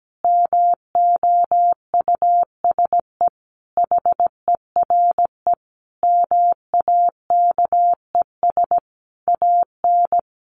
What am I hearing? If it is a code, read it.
Morse code, heard as a keyed tone, about 17 wpm, 700 Hz. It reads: MOUSE HERE MAKES AN
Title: A Sound Of Thunder 17wpm